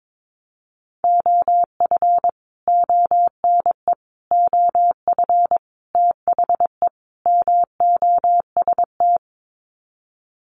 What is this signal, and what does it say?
Morse code, heard as a keyed tone, about 22 wpm, 700 Hz. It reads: OF ONE OF THE MOST
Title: Sets of 5 Words Encore2 22wpm